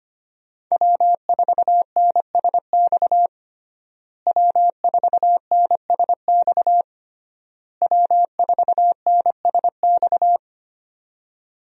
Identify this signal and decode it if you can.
Morse code, heard as a keyed tone, about 25 wpm, 700 Hz. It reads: W4NSX W4NSX W4NSX
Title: USA Call Signs - 5 Characters - Repeated 3x - 25wpm